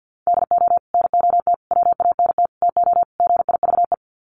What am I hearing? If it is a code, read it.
Morse code, heard as a keyed tone, about 50 wpm, 700 Hz. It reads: DO NOT WANT TO GIVE
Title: Sets of 5 Words Encore 50wpm